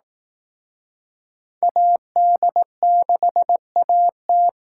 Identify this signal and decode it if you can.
Morse code, heard as a keyed tone, about 18 wpm, 700 Hz. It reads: AD6AT
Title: Speed Racing - USA 2x2 Call Signs 15wpm